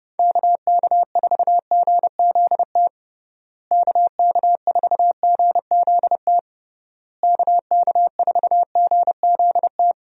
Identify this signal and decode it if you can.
Morse code, heard as a keyed tone, about 32 wpm, 700 Hz. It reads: KK4GZT KK4GZT KK4GZT
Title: USA Call Signs - 6 Characters - Repeated 3x - 30wpm